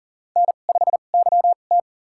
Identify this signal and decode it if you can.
Morse code, heard as a keyed tone, about 40 wpm, spaced as for 29 wpm, 700 Hz. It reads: N5YT
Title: USA Call Signs - 2x Character Spacing - 40wpm